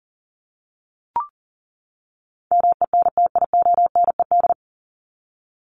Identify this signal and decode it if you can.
Morse code, heard as a keyed tone, about 40 wpm, 700 Hz. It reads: MENTIONED
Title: ED Suffix - Top 100 Words with Base Words 40wpm